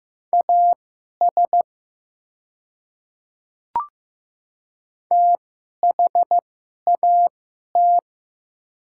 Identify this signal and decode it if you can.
Morse code, heard as a keyed tone, about 15 wpm, spaced as for 11 wpm, 700 Hz. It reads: AS THAT
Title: 2x Character Spacing - Top 100 Words - 15wpm